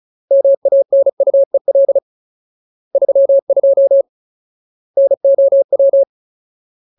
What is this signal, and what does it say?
Morse code, heard as a keyed tone, about 36 wpm, 550 Hz. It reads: MANUEL 32 NOW
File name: Ultimate Word List / 35WPM Ultimate